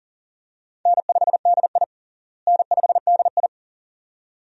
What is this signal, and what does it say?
Morse code, heard as a keyed tone, about 40 wpm, 700 Hz. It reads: N5DI N5DI
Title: USA Call Signs - 4 Characters - Repeated 2x - 40wpm